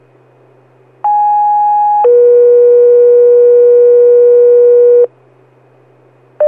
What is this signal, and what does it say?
Two-tone page, heard as a one-second tone, then a three-second tone, about 830 then 480 Hz. It is heard from 1.0 to 5.1 s.